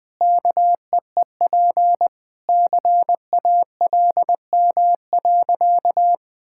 Morse code, heard as a keyed tone, about 20 wpm, 700 Hz. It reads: KEEP CALM.
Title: Sentences with 2-Word Phrases - 20wpm